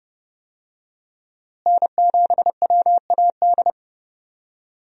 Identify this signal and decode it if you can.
Morse code, heard as a keyed tone, about 32 wpm, 700 Hz. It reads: N7WAD
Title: USA Call Signs - 5 Characters - Repeated 2x - 30wpm